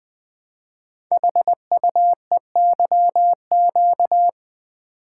Morse code, heard as a keyed tone, about 20 wpm, 700 Hz. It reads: HUEYQ
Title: Code Groups - 5 Letters - 20wpm